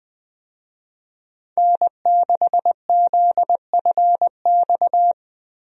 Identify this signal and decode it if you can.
Morse code, heard as a keyed tone, about 20 wpm, 700 Hz. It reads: N6ZFX
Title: USA Call Signs - 5 Characters - Repeated 3x - 20wpm